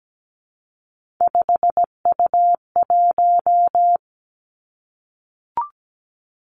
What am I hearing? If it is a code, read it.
Morse code, heard as a keyed tone, about 17 wpm, 700 Hz. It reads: 5U1
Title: Three Challenging Letter-Number Combinations 17wpm